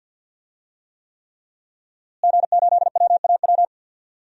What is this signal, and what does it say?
Morse code, heard as a keyed tone, about 47 wpm, 700 Hz. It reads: G8WAW
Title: DXCC-100 Call Signs - 5 Characters - Repeated 3x - 50wpm